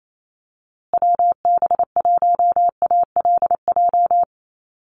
Morse code, heard as a keyed tone, about 28 wpm, 700 Hz. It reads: WB1ALJ
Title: USA Call Signs Repeated 2x 28wpm